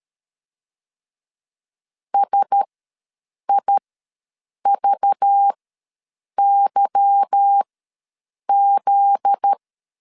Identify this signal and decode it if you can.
Morse code, heard as a keyed tone, about 13 wpm, spaced as for 7 wpm, 800 Hz. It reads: SIVYZ
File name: LEZIONI / 8 YJ45 / 8 YJ45 ws 12wpm 3146